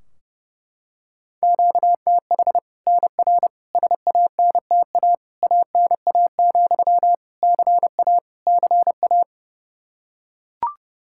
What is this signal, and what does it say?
Morse code, heard as a keyed tone, about 30 wpm, 700 Hz. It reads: QTH NR SANTA ANA, CA CA
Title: Practice QSO - Exchange 1 - Beginning - 30wpm